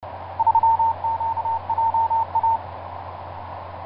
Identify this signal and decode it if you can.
Morse code, heard as a keyed tone, about 31 wpm, 910 Hz. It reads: 3QJA
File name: xu7acy ra3qja etc 10m cw 23feb2010 0951